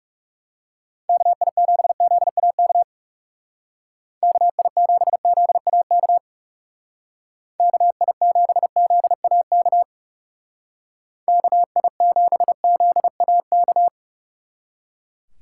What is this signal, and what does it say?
Morse code, heard as a keyed tone, about 38 wpm, 700 Hz. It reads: KI7ZAK KI7ZAK KI7ZAK KI7ZAK
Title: Speed Racing - USA 2x3 Call Signs 30wpm